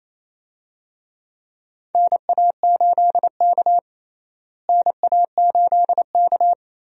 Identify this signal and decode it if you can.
Morse code, heard as a keyed tone, about 28 wpm, 700 Hz. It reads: NA8K NA8K
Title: USA Call Signs - 4 Characters - Repeated 3x - 28wpm